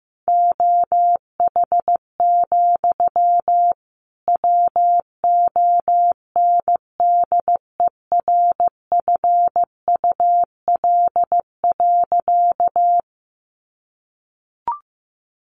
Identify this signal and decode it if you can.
Morse code, heard as a keyed tone, about 15 wpm, 700 Hz. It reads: OH, WONDERFUL.
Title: Sentences from Top 1100 Words 15wpm